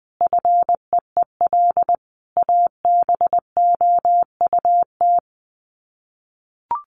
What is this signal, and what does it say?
Morse code, heard as a keyed tone, about 20 wpm, 700 Hz. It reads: FEEL ABOUT
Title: Sets of 5 Words 20wpm